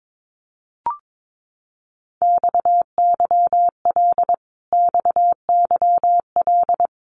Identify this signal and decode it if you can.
Morse code, heard as a keyed tone, about 22 wpm, 700 Hz. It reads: XYL XYL
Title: Abbreviations Repeated 3x - 22wpm